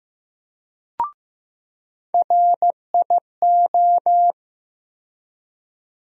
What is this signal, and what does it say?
Morse code, heard as a keyed tone, about 15 wpm, 700 Hz. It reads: RIO
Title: USA Call Sign Postfixes - 3 Letters 15wpm